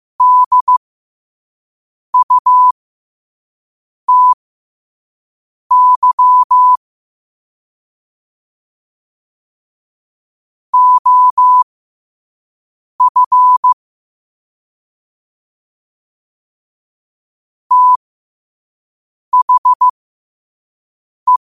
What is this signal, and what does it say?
Morse code, heard as a keyed tone, about 15 wpm, spaced as for 5 wpm, 1000 Hz. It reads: DUTY OF THE
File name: qotdfiles / 5wpm